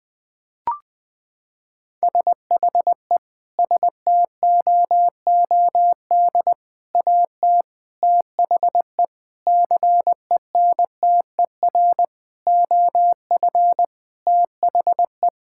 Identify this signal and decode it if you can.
Morse code, heard as a keyed tone, about 20 wpm, 700 Hz. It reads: SHE STOOD AT THE CENTER OF THE
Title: Sentences with 5-Word Phrases - 20wpm